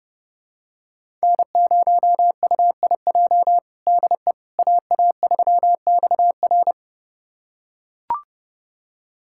Morse code, heard as a keyed tone, about 30 wpm, 700 Hz. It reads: N0UIJ DE AA3XR
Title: USA Callsign DE Callsign - 5 Characters - 30wpm